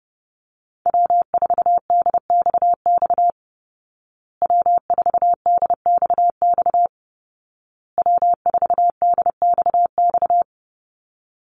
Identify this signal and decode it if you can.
Morse code, heard as a keyed tone, about 30 wpm, 700 Hz. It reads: W4DXX W4DXX W4DXX
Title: Call Signs Repeated 2x - 30wpm